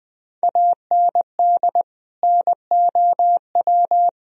Morse code, heard as a keyed tone, about 21 wpm, 700 Hz. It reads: AND NOW
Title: Sentences from Top 1000 Words - Review Part 2 - 20wpm